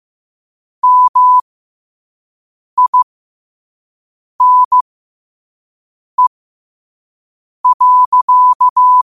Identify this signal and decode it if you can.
Morse code, heard as a keyed tone, about 15 wpm, spaced as for 5 wpm, 1000 Hz. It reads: MINE.